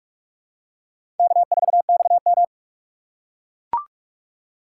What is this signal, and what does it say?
Morse code, heard as a keyed tone, about 45 wpm, 700 Hz. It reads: K4XM
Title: USA Call Signs - 4x Character Spacing - 45wpm